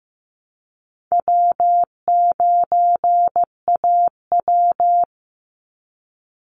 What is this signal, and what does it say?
Morse code, heard as a keyed tone, about 15 wpm, 700 Hz. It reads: W9AW
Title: Easy Call Signs Repeated 3x - 15wpm